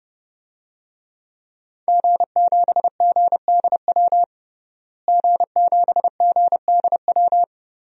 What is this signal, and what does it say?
Morse code, heard as a keyed tone, about 30 wpm, 700 Hz. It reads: G7GDW G7GDW
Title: DXCC-100 Call Signs - 5 Characters - Repeated 2x - 30wpm